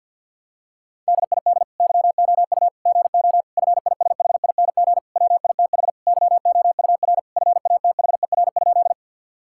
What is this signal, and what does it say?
Morse code, heard as a keyed tone, about 48 wpm, 700 Hz. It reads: DID YOU GO FISHING WITH YOUR FATHER?
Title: Sentences from Top 1000 Words - Review Part 5 - 50wpm